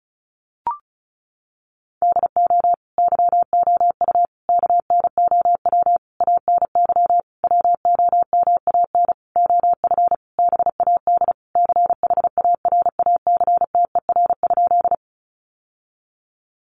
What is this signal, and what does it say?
Morse code, heard as a keyed tone, about 34 wpm, 700 Hz. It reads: DO YOU KNOW ANY WOMAN OF BAD CHARACTER?
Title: Sentences from Top 1200 Words 35wpm